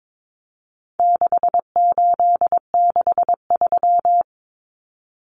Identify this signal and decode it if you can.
Morse code, heard as a keyed tone, about 22 wpm, 700 Hz. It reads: 6863
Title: Common Cut Numbers - 4-Digit 22wpm